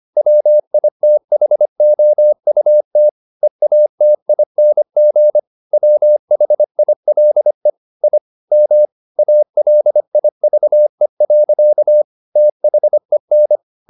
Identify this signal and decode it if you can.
Morse code, heard as a keyed tone, about 25 wpm, 600 Hz. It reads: WITHOUT EATING WHILE I M ALIVE. THEN
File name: the old man and the sea part 1 25wpm